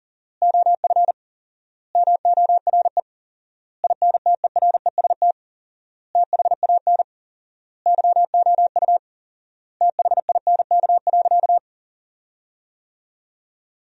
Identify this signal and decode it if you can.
Morse code, heard as a keyed tone, about 40 wpm, 700 Hz. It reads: OF MORE INTEREST THAN YOU THINK.
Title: 4x Word-Spacing - Sentences from Top 400 Words - 40wpm